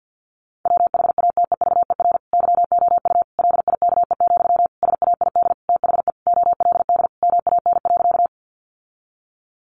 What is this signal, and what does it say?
Morse code, heard as a keyed tone, about 50 wpm, 700 Hz. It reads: WHATEVER YOU LIKE, SAID THE OLD MAN.
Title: Sentences from Top 1100 Words 50wpm